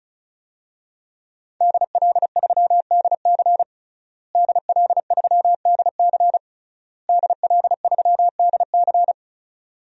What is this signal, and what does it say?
Morse code, heard as a keyed tone, about 36 wpm, 700 Hz. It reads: DL3DC DL3DC DL3DC
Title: DXCC-100 Call Signs - 5 Characters - Repeated 3x - 35wpm